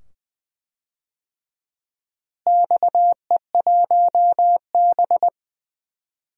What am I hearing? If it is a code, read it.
Morse code, heard as a keyed tone, about 21 wpm, 700 Hz. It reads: XE1B
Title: DXCC-100 Call Signs - 4 Characters - Repeated 2x - 20wpm